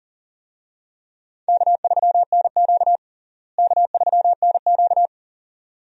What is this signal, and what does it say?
Morse code, heard as a keyed tone, about 40 wpm, 700 Hz. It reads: K3NQ K3NQ
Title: USA Call Signs - 4 Characters - Repeated 2x - 40wpm